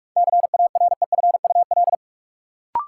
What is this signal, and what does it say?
Morse code, heard as a keyed tone, about 47 wpm, 700 Hz. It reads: CAREFUL
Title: Sentences from Top 1600 Words - New Word Drill - 45wpm